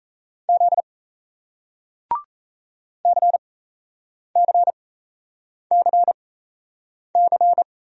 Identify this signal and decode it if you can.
Morse code, heard as a keyed tone, about 38 wpm, spaced as for 8 wpm, 700 Hz. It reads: Z CCCC